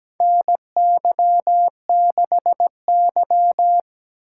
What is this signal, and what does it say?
Morse code, heard as a keyed tone, about 17 wpm, 700 Hz. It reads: NY6Y
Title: K1USN SST QSO Practice - 17wpm